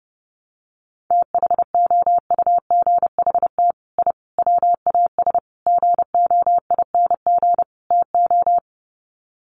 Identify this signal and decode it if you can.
Morse code, heard as a keyed tone, about 30 wpm, 700 Hz. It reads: THOUGHT I WAS GOING TO
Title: Sets of 5 Words Encore2 30wpm